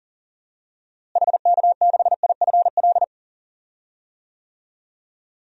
Morse code, heard as a keyed tone, about 40 wpm, 700 Hz. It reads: HK6IFL
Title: DXCC-100 Call Signs - 6 Characters - Repeated 2x - 40wpm